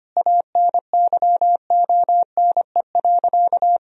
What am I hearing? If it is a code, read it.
Morse code, heard as a keyed tone, about 25 wpm, 700 Hz. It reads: ANYONE.
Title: Sentences from Top 1000 Words - Review Part 2 - 25wpm